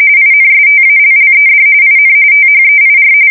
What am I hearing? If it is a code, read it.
RTTY, about 45.45 baud. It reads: INSTALLED IN THEIR
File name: fsk170 / USS-Missouri